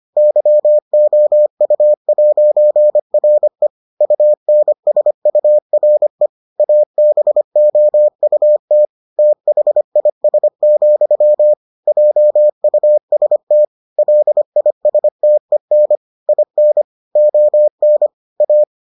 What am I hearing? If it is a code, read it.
Morse code, heard as a keyed tone, about 25 wpm, 600 Hz. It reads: YOU'RE UNSURE ABOUT THIS, JUST LISTEN IN ON A